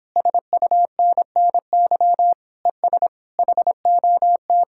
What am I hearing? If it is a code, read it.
Morse code, heard as a keyed tone, about 26 wpm, 700 Hz. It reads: SUNNY ES HOT